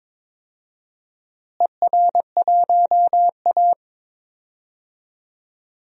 Morse code, heard as a keyed tone, about 22 wpm, 700 Hz. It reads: ER1A
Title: DXCC-100 Call Signs - 4 Characters - Repeated 3x - 22wpm